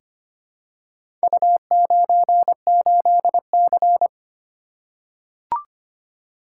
Morse code, heard as a keyed tone, about 25 wpm, 700 Hz. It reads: U98C
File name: practice / 25wpm / Four Letter-Number Combinations - 25wpm